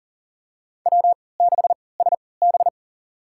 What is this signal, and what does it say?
Morse code, heard as a keyed tone, about 40 wpm, spaced as for 23 wpm, 700 Hz. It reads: W6SB
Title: USA Call Signs - 3x Character Spacing - 40wpm